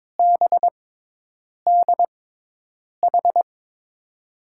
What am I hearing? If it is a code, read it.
Morse code, heard as a keyed tone, about 22 wpm, spaced as for 8 wpm, 700 Hz. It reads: BDH
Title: USA Call Signs - 5 Characters - 6x Character Spacing - 22wpm